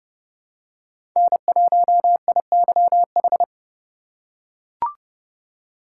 Morse code, heard as a keyed tone, about 30 wpm, 700 Hz. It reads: N1IYH
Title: USA Call Signs - 5 Characters - 3x Character Spacing - 30wpm